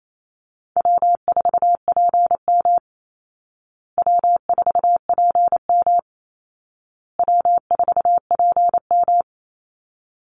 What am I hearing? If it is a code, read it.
Morse code, heard as a keyed tone, about 28 wpm, 700 Hz. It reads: W4PM W4PM W4PM
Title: Easy Call Signs Repeated 3x - 28wpm